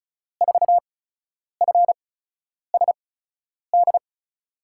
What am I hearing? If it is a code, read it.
Morse code, heard as a keyed tone, about 36 wpm, spaced as for 10 wpm, 700 Hz. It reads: 4FSD
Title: USA Call Signs - 6 Characters - 8x Character Spacing - 35wpm